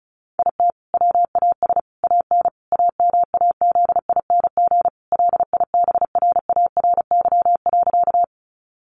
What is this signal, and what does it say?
Morse code, heard as a keyed tone, about 34 wpm, 700 Hz. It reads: IT WAS AN AMAZING LIBRARY.
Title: Sentences from Top 1500 Words 35wpm